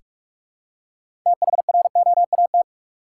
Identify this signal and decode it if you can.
Morse code, heard as a keyed tone, about 46 wpm, 700 Hz. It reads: THROAT
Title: Six Letter Words - 45wpm